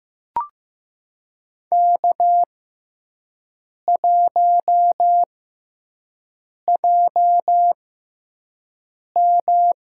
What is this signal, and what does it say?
Morse code, heard as a keyed tone, about 15 wpm, spaced as for 5 wpm, 700 Hz. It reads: K1JM
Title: USA Call Signs - 4 Characters - 6x Character Spacing - 15wpm